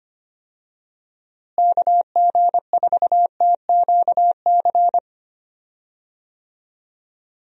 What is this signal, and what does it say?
Morse code, heard as a keyed tone, about 25 wpm, 700 Hz. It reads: KG4TQC
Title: USA Call Signs - 6 Characters - Repeated 2x - 25wpm